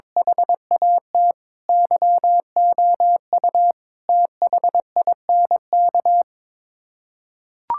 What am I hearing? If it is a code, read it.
Morse code, heard as a keyed tone, about 22 wpm, 700 Hz. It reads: HAT YOU THINK
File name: practice / 22wpm / Sets of 6 Words - 22wpm